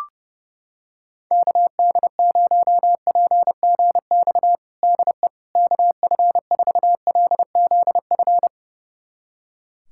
Morse code, heard as a keyed tone, about 30 wpm, 700 Hz. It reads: KD0PGX DE KF4LZF
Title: USA Callsign DE Callsign 30wpm